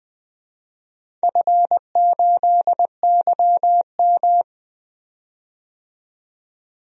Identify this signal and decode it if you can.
Morse code, heard as a keyed tone, about 20 wpm, 700 Hz. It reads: F8YM
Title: DXCC-100 Call Signs - 4 Characters - 20wpm